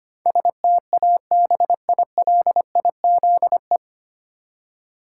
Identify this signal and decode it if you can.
Morse code, heard as a keyed tone, about 25 wpm, 700 Hz. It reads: STABILIZE
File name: Nine Letter Words - 25wpm